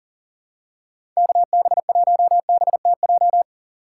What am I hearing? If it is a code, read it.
Morse code, heard as a keyed tone, about 40 wpm, 700 Hz. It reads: KB1BTJ
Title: USA Call Signs - 2x Character Spacing - 40wpm